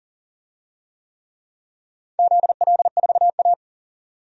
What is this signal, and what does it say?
Morse code, heard as a keyed tone, about 37 wpm, 700 Hz. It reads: ZL4A
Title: DXCC-100 Call Signs - 4 Characters - 40wpm